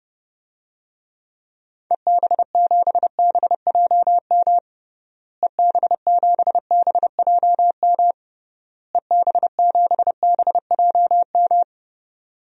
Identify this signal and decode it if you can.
Morse code, heard as a keyed tone, about 32 wpm, 700 Hz. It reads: EB7BJM EB7BJM EB7BJM
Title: DXCC-100 Call Signs - 6 Characters - Repeated 3x - 30wpm